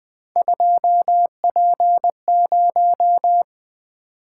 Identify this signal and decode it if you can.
Morse code, heard as a keyed tone, about 20 wpm, 700 Hz. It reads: 2P0
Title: Three Letter-Number Combinations - 20wpm